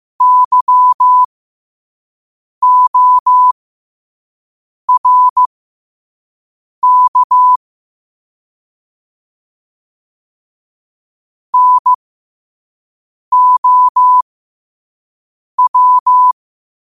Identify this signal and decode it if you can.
Morse code, heard as a keyed tone, about 15 wpm, spaced as for 5 wpm, 1000 Hz. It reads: YORK NOW
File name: qotdfiles / 5wpm